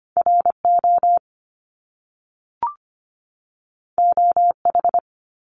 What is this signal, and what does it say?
Morse code, heard as a keyed tone, about 25 wpm, 700 Hz. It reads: RO OH
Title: Top Two Letter Combinations in Words 25wpm